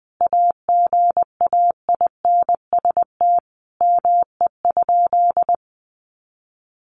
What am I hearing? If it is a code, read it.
Morse code, heard as a keyed tone, about 20 wpm, 700 Hz. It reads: AGAINST ME?
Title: Sentences from Top 600 Words 20wpm